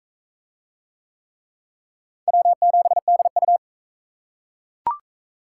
Morse code, heard as a keyed tone, about 41 wpm, 700 Hz. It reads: W7DU